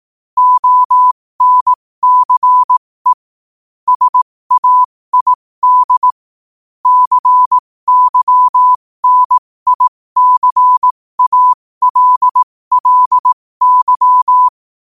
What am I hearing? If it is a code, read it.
Morse code, heard as a keyed tone, about 18 wpm, spaced as for 15 wpm, 1000 Hz. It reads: ONCE SAID CYNICALLY
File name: qotdfiles / 15wpm